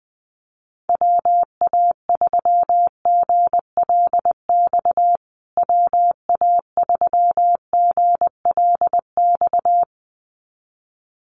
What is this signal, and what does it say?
Morse code, heard as a keyed tone, about 19 wpm, 700 Hz. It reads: WA3GLX WA3GLX
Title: USA Call Signs Repeated 2x 20wpm